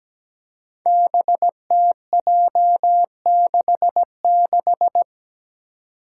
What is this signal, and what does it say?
Morse code, heard as a keyed tone, about 17 wpm, 700 Hz. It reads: BTJ66
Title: Five Challenging Letter-Number Combinations 17wpm